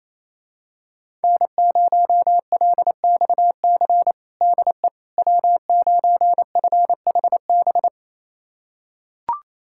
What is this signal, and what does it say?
Morse code, heard as a keyed tone, about 28 wpm, 700 Hz. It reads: N0LXC DE W9FHB
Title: USA Callsign DE Callsign - 5 Characters - 28wpm